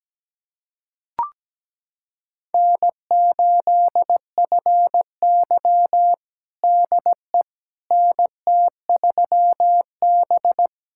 Morse code, heard as a keyed tone, about 17 wpm, 700 Hz. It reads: N8FY DE NT3B
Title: USA Callsign DE Callsign - 4 Characters - 17wpm